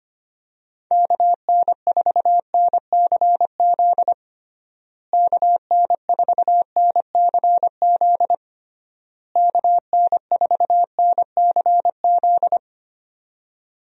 Morse code, heard as a keyed tone, about 25 wpm, 700 Hz. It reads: KN4NCZ KN4NCZ KN4NCZ
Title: USA Call Signs - 6 Characters - Repeated 3x - 25wpm